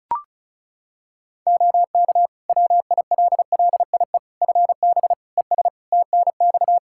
Morse code, heard as a keyed tone, about 36 wpm, 700 Hz. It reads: OK WILLIE FB ES TNX
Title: Practice QSO - Exchange 2 - Info - 35wpm